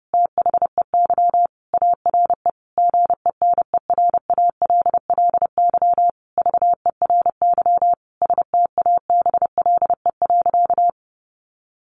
Morse code, heard as a keyed tone, about 30 wpm, 700 Hz. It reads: THEY ARE GENERALLY VERY STABLE.
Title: Sentences from Top 1500 Words 30wpm